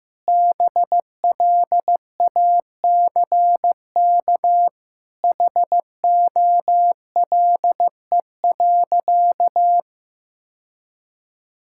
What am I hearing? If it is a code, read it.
Morse code, heard as a keyed tone, about 16 wpm, 700 Hz. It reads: BLACK HOLE.
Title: Sentences from Top 1600 Words 15wpm